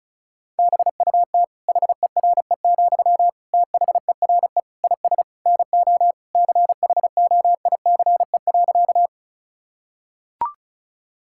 Morse code, heard as a keyed tone, about 34 wpm, 700 Hz. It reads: BUT HERE, THERE IS NO CHOICE.
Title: Sentences from Top 1000 Words - Review Part 6 - 35wpm